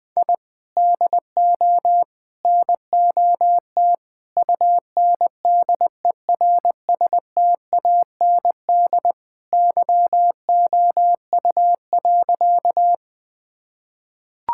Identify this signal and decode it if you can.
Morse code, heard as a keyed tone, about 20 wpm, 700 Hz. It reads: I DO NOT UNDERSTAND YOU.
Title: Sentences with 4-Word Phrases - 20wpm